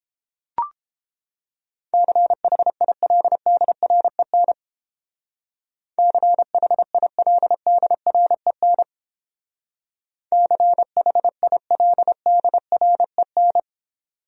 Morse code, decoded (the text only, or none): CHILDREN CHILDREN CHILDREN